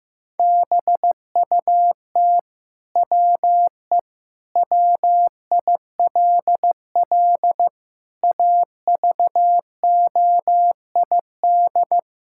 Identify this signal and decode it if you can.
Morse code, heard as a keyed tone, about 15 wpm, 700 Hz. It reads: BUT WE WILL AVOID
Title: Sentences from Top 1000 Words - Review Part 6 - 15wpm